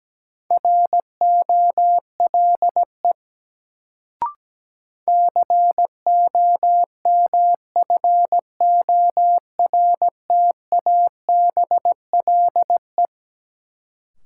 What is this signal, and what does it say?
Morse code, heard as a keyed tone, about 17 wpm, 700 Hz. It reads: ROLE COMFORTABLE
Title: Sentences from Top 1500 Words - New Word Drill Alt - 17wpm